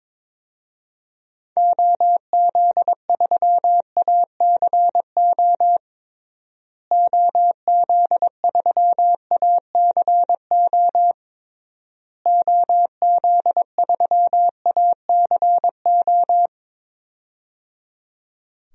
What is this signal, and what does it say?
Morse code, heard as a keyed tone, about 22 wpm, 700 Hz. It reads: OZ3ACO OZ3ACO OZ3ACO
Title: DXCC-100 Call Signs - 6 Characters - Repeated 3x - 22wpm